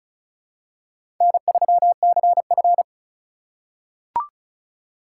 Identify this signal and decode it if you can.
Morse code, heard as a keyed tone, about 36 wpm, 700 Hz. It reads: N3CF
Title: USA Call Signs - 4 Characters - 2x Character Spacing - 35wpm